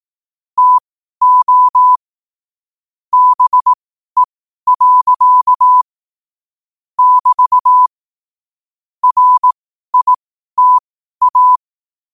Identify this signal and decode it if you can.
Morse code, heard as a keyed tone, about 18 wpm, spaced as for 12 wpm, 1000 Hz. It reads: TO BE. <BT> RITA
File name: qotdfiles / 12wpm